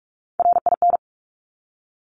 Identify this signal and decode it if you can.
Morse code, heard as a keyed tone, about 46 wpm, 700 Hz. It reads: RIN
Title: Top Three Letter Combinations in Words 45wpm